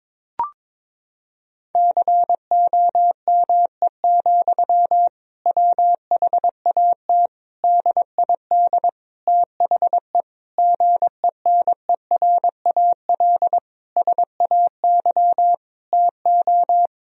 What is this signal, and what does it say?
Morse code, heard as a keyed tone, about 22 wpm, 700 Hz. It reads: COME, WHAT DID THE GENERAL SAY TO
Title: Sentences from Top 1000 Words - Review Part 3 - 22wpm